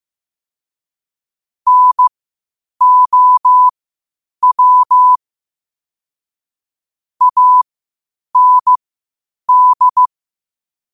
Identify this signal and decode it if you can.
Morse code, heard as a keyed tone, about 15 wpm, spaced as for 8 wpm, 1000 Hz. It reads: NOW AND